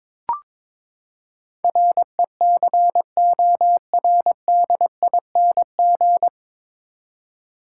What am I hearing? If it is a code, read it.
Morse code, heard as a keyed tone, about 21 wpm, 700 Hz. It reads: RECORDING
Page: Morse Code Ninja practice 22wpm